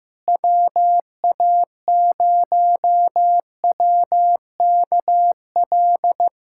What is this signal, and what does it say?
Morse code, heard as a keyed tone, about 15 wpm, 700 Hz. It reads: WA0WKL
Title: USA Callsign DE Callsign 15wpm